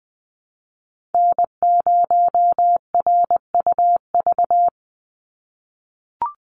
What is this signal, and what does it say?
Morse code, heard as a keyed tone, about 20 wpm, 700 Hz. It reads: N0RUV
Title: USA Call Signs 20wpm